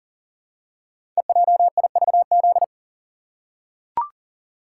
Morse code, heard as a keyed tone, about 41 wpm, 700 Hz. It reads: EJIVZ
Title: Code Groups - 5 Letters - 40wpm